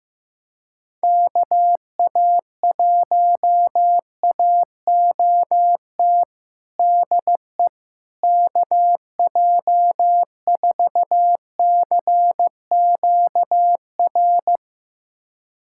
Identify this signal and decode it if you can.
Morse code, heard as a keyed tone, about 15 wpm, 700 Hz. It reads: KA1AOT DE KJ4CQR
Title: USA Callsign DE Callsign 15wpm